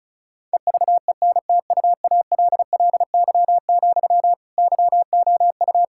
Morse code, heard as a keyed tone, about 36 wpm, 700 Hz. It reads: EVENTUALLY, YOU
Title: Sentences from Top 1000 Words - Review Part 6 - 35wpm